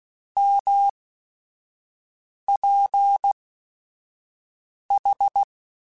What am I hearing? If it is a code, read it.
Morse code, heard as a keyed tone, about 16 wpm, spaced as for 5 wpm, 800 Hz. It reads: MPH
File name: Weather Sets / Weather Sets-mp3s / Weather 4 14 @5